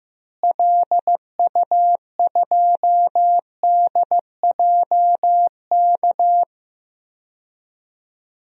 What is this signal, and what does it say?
Morse code, heard as a keyed tone, about 15 wpm, 700 Hz. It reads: LU2DJK
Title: DXCC-100 Call Signs - 6 Characters - Repeated 3x - 15wpm